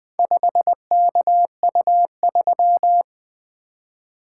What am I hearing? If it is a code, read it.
Morse code, heard as a keyed tone, about 20 wpm, 700 Hz. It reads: 5KU3
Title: Four Letter-Number Combinations - 20wpm